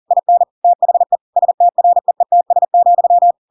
Morse code, heard as a keyed tone, about 40 wpm, 700 Hz. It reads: IN THE STREETS,